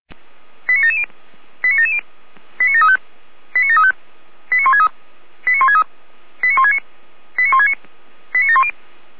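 Five-tone selective call (ZVEI), heard as quick runs of five tones, tones of about 80 ms. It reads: 89770 89770 89735 89735 89163 89163 89169 89169 89710
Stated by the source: ZVEI five-tone selective call received via SDR.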